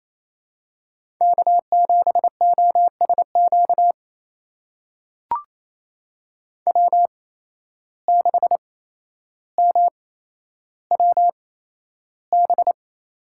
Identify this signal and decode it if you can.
Morse code, heard as a keyed tone, about 28 wpm, spaced as for 12 wpm, 700 Hz. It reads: K7OSQ W6MWB
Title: USA Call Signs - 5 Characters - 8x Character Spacing - 28wpm